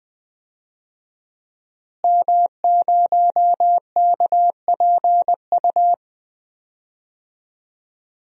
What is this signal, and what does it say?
Morse code, heard as a keyed tone, about 20 wpm, 700 Hz. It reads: M0KPU
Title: DXCC-100 Call Signs - 5 Characters - Repeated 3x - 20wpm